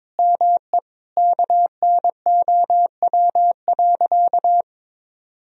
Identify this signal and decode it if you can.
Morse code, heard as a keyed tone, about 22 wpm, 700 Hz. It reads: ME KNOW.
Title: Sentences from Top 1000 Words - Review Part 2 - 22wpm